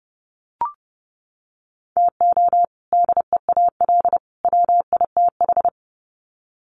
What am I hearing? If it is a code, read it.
Morse code, heard as a keyed tone, about 30 wpm, 700 Hz. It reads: TO DEAL WITH
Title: Sets of 3 Words 30wpm